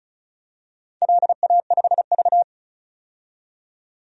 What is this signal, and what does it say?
Morse code, heard as a keyed tone, about 36 wpm, 700 Hz. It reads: LA5V
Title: DXCC-100 Call Signs - 4 Characters - 35wpm